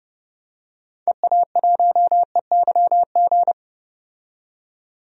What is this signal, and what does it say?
Morse code, heard as a keyed tone, about 31 wpm, 700 Hz. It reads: EA1EYG
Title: DXCC-100 Call Signs - 6 Characters - 30wpm